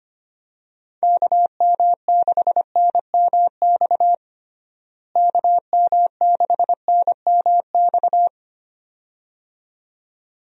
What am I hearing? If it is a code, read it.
Morse code, heard as a keyed tone, about 25 wpm, 700 Hz. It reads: KM6NMX KM6NMX
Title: USA Call Signs - 6 Characters - Repeated 2x - 25wpm